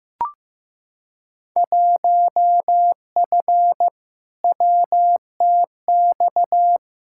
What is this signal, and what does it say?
Morse code, heard as a keyed tone, about 15 wpm, 700 Hz. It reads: 1F WTX
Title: Field Day Exchanges - 15wpm